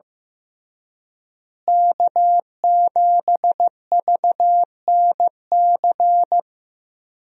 Morse code, heard as a keyed tone, about 15 wpm, 700 Hz. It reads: K7VNC
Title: USA Call Signs - 5 Characters - Repeated 3x - 15wpm